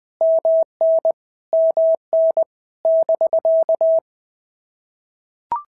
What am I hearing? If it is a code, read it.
Morse code, heard as a keyed tone, about 20 wpm, 650 Hz. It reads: MN MN <BK>
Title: POTA QSO Practice - 20wpm